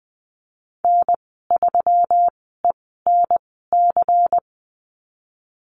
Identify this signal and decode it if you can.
Morse code, heard as a keyed tone, about 20 wpm, spaced as for 15 wpm, 700 Hz. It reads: N3ENC
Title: 154 Lesson 25g - Letter p - Call Signs 20wpm